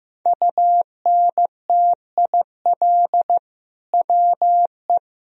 Morse code, heard as a keyed tone, about 15 wpm, 700 Hz. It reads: UNTIL WE
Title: Sentences from Top 1000 Words - Review Part 1 - 15wpm